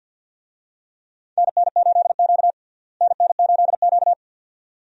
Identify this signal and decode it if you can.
Morse code, heard as a keyed tone, about 48 wpm, 700 Hz. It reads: NN8Q NN8Q
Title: USA Call Signs - 4 Characters - Repeated 2x - 50wpm